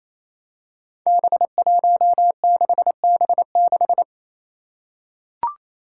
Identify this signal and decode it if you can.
Morse code, heard as a keyed tone, about 27 wpm, 700 Hz. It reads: B16B6
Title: Five Challenging Letter-Number Combinations 28wpm